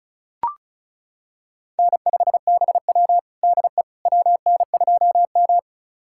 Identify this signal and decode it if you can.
Morse code, heard as a keyed tone, about 36 wpm, 700 Hz. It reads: N5BW DE WN2M
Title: USA Callsign DE Callsign - 4 Characters - 35wpm